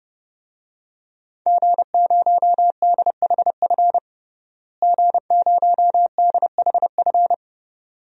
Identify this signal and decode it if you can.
Morse code, heard as a keyed tone, about 30 wpm, 700 Hz. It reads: G0DHF G0DHF
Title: DXCC-100 Call Signs - 5 Characters - Repeated 2x - 30wpm